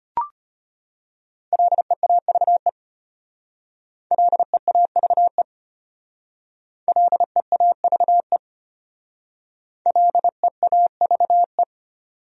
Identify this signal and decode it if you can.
Morse code, heard as a keyed tone, about 31 wpm, 700 Hz. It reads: LEAVE LEAVE LEAVE LEAVE
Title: Speed-Racing Top 500 Words 25wpm